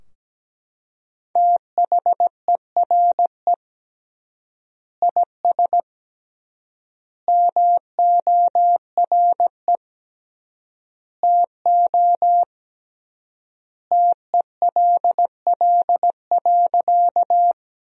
Morse code, heard as a keyed tone, about 17 wpm, 700 Hz. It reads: THERE IS MORE TO TELL.
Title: 3x Word-Spacing - Sentences from Top 200 Words - 17wpm